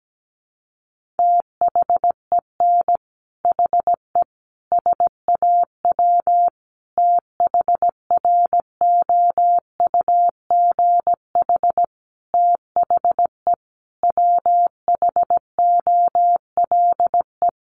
Morse code, heard as a keyed tone, about 17 wpm, 700 Hz. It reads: THEN HE SAW THROUGH THE WHOLE
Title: Sentences from Top 400 Words 17wpm